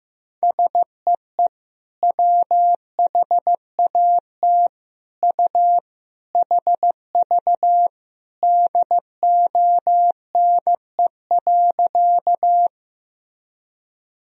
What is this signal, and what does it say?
Morse code, heard as a keyed tone, about 15 wpm, 700 Hz. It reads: SEE WHAT U HV DONE.
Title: Sentences from Top 500 Words and Abbreviations 15wpm